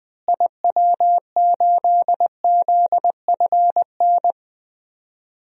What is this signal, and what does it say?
Morse code, heard as a keyed tone, about 21 wpm, 700 Hz. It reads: IW8ZFN
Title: DXCC-100 Call Signs - 6 Characters - Repeated 3x - 20wpm